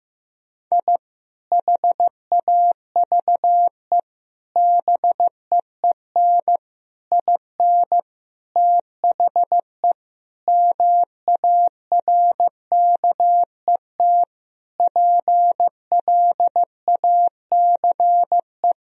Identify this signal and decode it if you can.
Morse code, heard as a keyed tone, about 15 wpm, 700 Hz. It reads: I HAVE BEEN IN THE MARKET PLACE
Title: Sentences from Top 1000 Words - Review Part 3 - 15wpm